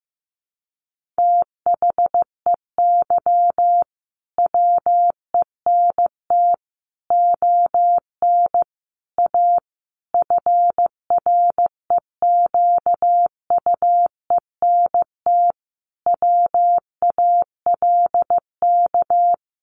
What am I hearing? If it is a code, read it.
Morse code, heard as a keyed tone, about 15 wpm, 700 Hz. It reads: THEY WENT ON A FREQUENT WALK
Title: Sentences from Top 1400 Words 15wpm